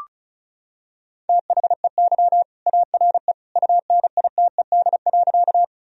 Morse code, heard as a keyed tone, about 34 wpm, 700 Hz. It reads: THEY ARE UNITED.
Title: Sentences from Top 1000 Words - Review Part 4 - 35wpm